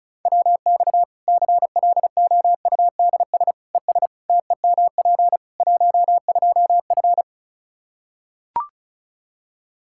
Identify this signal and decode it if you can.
Morse code, heard as a keyed tone, about 36 wpm, 700 Hz. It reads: WX CLOUDS ES TEMP 12F
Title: K1USN SST QSO Practice 35wpm